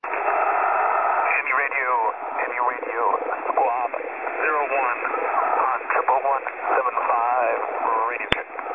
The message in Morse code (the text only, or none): none